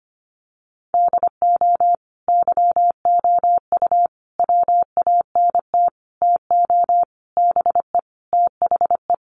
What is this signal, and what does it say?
Morse code, heard as a keyed tone, about 25 wpm, 700 Hz. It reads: DO YOU WANT TO BE THE
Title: Sentences from Top 1100 Words 25wpm